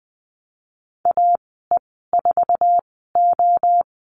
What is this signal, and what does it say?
Morse code, heard as a keyed tone, about 19 wpm, spaced as for 14 wpm, 700 Hz. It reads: AE4O
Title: 055 Lesson 10g - Letter r - Call Signs 20wpm